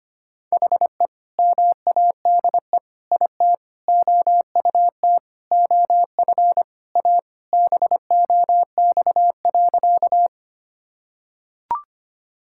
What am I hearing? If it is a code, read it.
Morse code, heard as a keyed tone, about 25 wpm, 700 Hz. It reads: HE MADE IT OUT OF A BOX.
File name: practice / 25wpm / Sentences with 3-Word Phrases - 25wpm